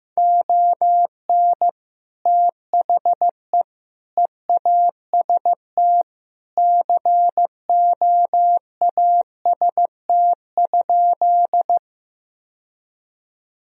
Morse code, heard as a keyed tone, about 15 wpm, 700 Hz. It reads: ON THE EAST COAST?
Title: Sentences from Top 1000 Words - Review Part 4 - 15wpm